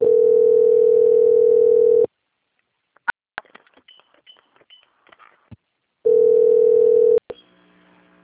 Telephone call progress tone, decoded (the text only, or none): ringback tone